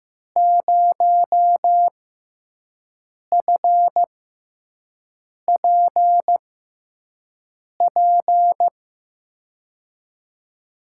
Morse code, heard as a keyed tone, about 15 wpm, spaced as for 5 wpm, 700 Hz. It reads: 0FPP
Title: USA Call Signs - 5 Characters - 6x Character Spacing - 15wpm